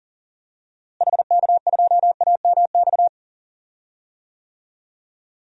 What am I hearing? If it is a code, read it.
Morse code, heard as a keyed tone, about 38 wpm, 700 Hz. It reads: HK2AMX
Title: DXCC-100 Call Signs - 6 Characters - Repeated 2x - 40wpm